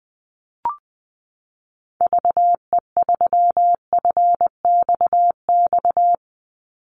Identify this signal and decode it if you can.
Morse code, heard as a keyed tone, about 20 wpm, 700 Hz. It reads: VE3FXX
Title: Call Signs Repeated 2x - 20wpm